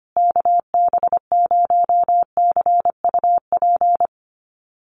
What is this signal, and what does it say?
Morse code, heard as a keyed tone, about 25 wpm, 700 Hz. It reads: KB0CUP
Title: USA Call Signs Repeated 3x 25wpm